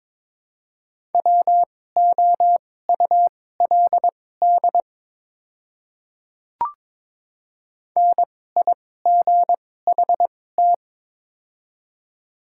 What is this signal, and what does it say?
Morse code, heard as a keyed tone, about 22 wpm, spaced as for 16 wpm, 700 Hz. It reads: WOULD NIGHT
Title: 2x Character Spacing - Top 300 Words - 22wpm